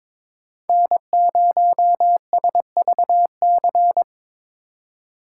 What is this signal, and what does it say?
Morse code, heard as a keyed tone, about 26 wpm, 700 Hz. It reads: N0SVC
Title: Speed Racing - USA 1x3 Call Signs 22wpm